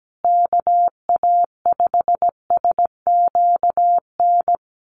Morse code, heard as a keyed tone, about 17 wpm, 700 Hz. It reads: KA5SQN
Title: USA Call Signs Repeated 3x 17wpm